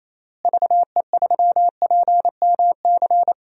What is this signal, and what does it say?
Morse code, heard as a keyed tone, about 28 wpm, 700 Hz. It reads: VE3PMC
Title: DXCC-100 Call Signs - 6 Characters - Repeated 3x - 28wpm